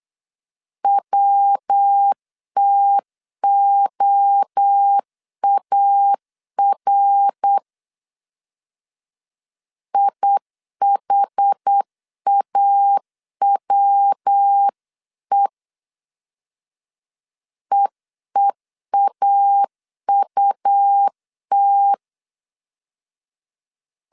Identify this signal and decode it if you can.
Morse code, heard as a keyed tone, about 8 wpm, 800 Hz. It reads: WTOAR IHAWE EEAUT
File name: CW / DET / 1 3 1500C S C 8wpm 4214